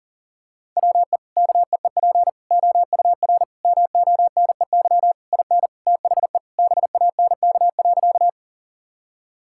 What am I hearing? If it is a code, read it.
Morse code, heard as a keyed tone, about 40 wpm, 700 Hz. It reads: WE KEEP OUR MONEY IN THE BANK.
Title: Sentences from Top 1000 Words - Review Part 6 - 40wpm